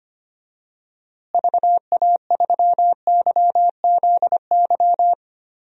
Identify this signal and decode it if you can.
Morse code, heard as a keyed tone, about 25 wpm, 700 Hz. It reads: VA3YZY
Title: DXCC-100 Call Signs - 6 Characters - Repeated 2x - 25wpm